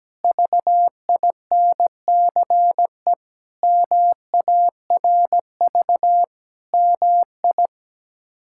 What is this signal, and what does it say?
Morse code, heard as a keyed tone, about 17 wpm, 700 Hz. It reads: VINCE MARV MI
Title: K1USN SST QSO Practice - 17wpm